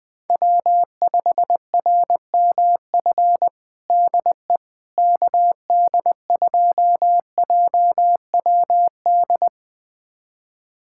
Morse code, heard as a keyed tone, about 20 wpm, 700 Hz. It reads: W5RMF DE KD2JWD
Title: USA Callsign DE Callsign 20wpm